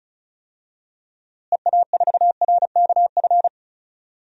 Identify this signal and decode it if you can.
Morse code, heard as a keyed tone, about 34 wpm, 700 Hz. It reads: EA4RKF
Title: DXCC-100 Call Signs - 6 Characters - Repeated 3x - 35wpm